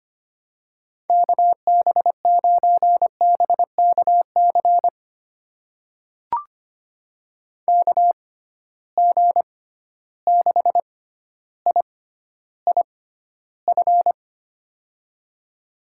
Morse code, heard as a keyed tone, about 25 wpm, spaced as for 13 wpm, 700 Hz. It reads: KB9BKC KG6IIF
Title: USA Call Signs - 6 Characters - 6x Character Spacing - 25wpm